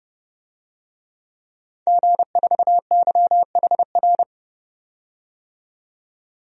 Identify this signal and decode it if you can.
Morse code, heard as a keyed tone, about 30 wpm, 700 Hz. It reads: G4YHR
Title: DXCC-100 Call Signs - 5 Characters - 30wpm